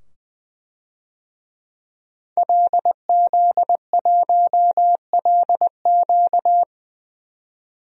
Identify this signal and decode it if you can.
Morse code, heard as a keyed tone, about 20 wpm, 700 Hz. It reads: LZ1LQ
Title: DXCC-100 Call Signs - 5 Characters - 20wpm